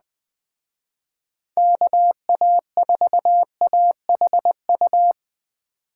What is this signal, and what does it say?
Morse code, heard as a keyed tone, about 20 wpm, 700 Hz. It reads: KA4AHU
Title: Speed Racing - USA 2x3 Call Signs 17wpm